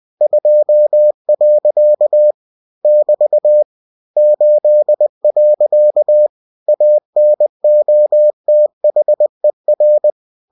Morse code, heard as a keyed tone, about 20 wpm, 600 Hz. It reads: 2. <BT> 8. ANOTHER
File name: cw-books / winnie-the-pooh / winnie-20wpm-page-0064